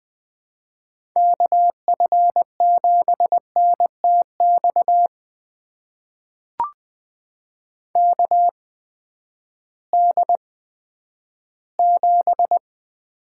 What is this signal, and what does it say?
Morse code, heard as a keyed tone, about 21 wpm, spaced as for 12 wpm, 700 Hz. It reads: KF7NTX KD7
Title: USA Call Signs - 6 Characters - 8x Character Spacing - 20wpm